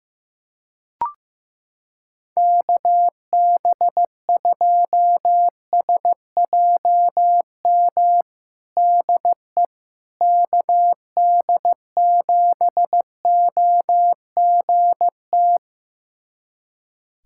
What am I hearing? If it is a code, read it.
Morse code, heard as a keyed tone, about 15 wpm, 700 Hz. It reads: KB2SJM DE KD7OGT
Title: USA Callsign DE Callsign - 6 Characters - 15wpm